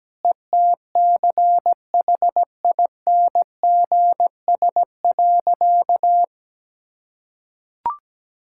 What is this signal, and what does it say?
Morse code, heard as a keyed tone, about 17 wpm, 700 Hz. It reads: ETCHINGS.
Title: Pangram Challenge v2 - 17wpm